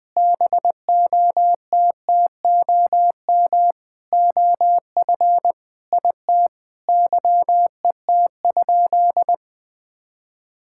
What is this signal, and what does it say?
Morse code, heard as a keyed tone, about 20 wpm, 700 Hz. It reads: BOTTOM OF IT YET?
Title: Sentences from Top 1000 Words - Review Part 6 - 20wpm